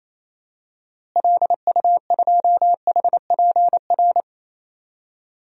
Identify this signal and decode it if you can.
Morse code, heard as a keyed tone, about 28 wpm, 700 Hz. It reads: LU2HPR
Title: DXCC-100 Call Signs - 6 Characters - Repeated 2x - 28wpm